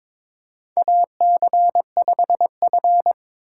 Morse code, heard as a keyed tone, about 21 wpm, 700 Hz. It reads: AC5F
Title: USA Call Signs - 2x Character Spacing - 22wpm